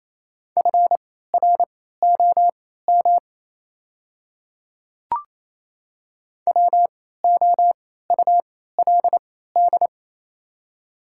Morse code, heard as a keyed tone, about 27 wpm, spaced as for 16 wpm, 700 Hz. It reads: FROM WOULD
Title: 3x Character Spacing - Top 200 Words - 28wpm